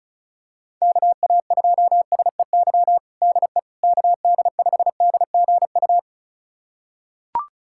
Morse code, heard as a keyed tone, about 34 wpm, 700 Hz. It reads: KA2SEY DE KD5DGU
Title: USA Callsign DE Callsign 35wpm